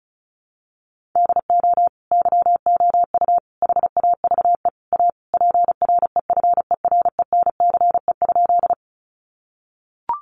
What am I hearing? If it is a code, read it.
Morse code, heard as a keyed tone, about 34 wpm, 700 Hz. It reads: DO YOU HAVE A PREFERENCE?
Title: Sentences from Top 1500 Words 35wpm